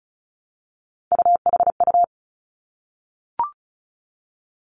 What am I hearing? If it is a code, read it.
Morse code, heard as a keyed tone, about 36 wpm, 700 Hz. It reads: UHU
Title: Three Challenging Letter-Number Combinations 35wpm